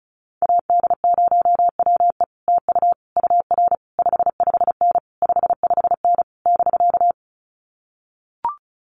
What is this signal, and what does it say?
Morse code, heard as a keyed tone, about 34 wpm, 700 Hz. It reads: AD0WE TU UR 55N 55N <BK>
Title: Common QSO Phrases Encore 35wpm